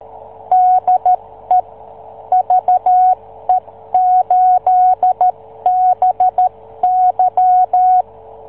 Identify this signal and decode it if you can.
Morse code, heard as a keyed tone, about 13 wpm, 740 Hz. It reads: DE VE8BY